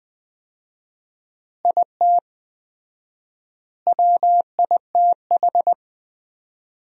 Morse code, heard as a keyed tone, about 21 wpm, 700 Hz. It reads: IT WITH
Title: 4x Word-Spacing - Sentences from Top 500 Words - 20wpm